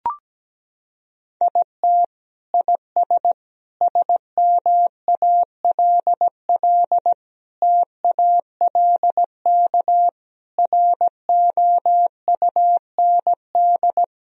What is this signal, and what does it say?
Morse code, heard as a keyed tone, about 17 wpm, 700 Hz. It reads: IT IS SMALL TALK ROUND
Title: Sentences from Top 1000 Words - Review Part 1 - 17wpm